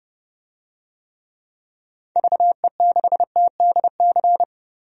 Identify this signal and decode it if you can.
Morse code, heard as a keyed tone, about 30 wpm, 700 Hz. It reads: VE6TDC
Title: DXCC-100 Call Signs - 6 Characters - 30wpm